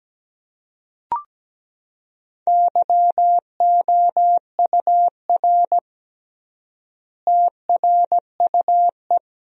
Morse code, heard as a keyed tone, about 17 wpm, 700 Hz. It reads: YOUR TRUE
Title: 3x Word-Spacing - Sentences from Top 400 Words - 17wpm